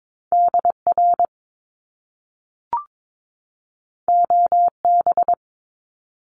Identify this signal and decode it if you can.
Morse code, heard as a keyed tone, about 22 wpm, 700 Hz. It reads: DR OB
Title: Top Two Letter Combinations in Words 22wpm